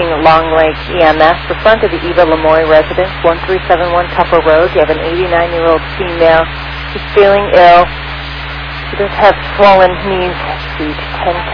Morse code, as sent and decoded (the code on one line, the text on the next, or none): none